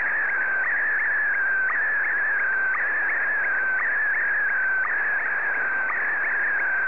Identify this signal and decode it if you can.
SSTV (slow-scan television), Scottie DX.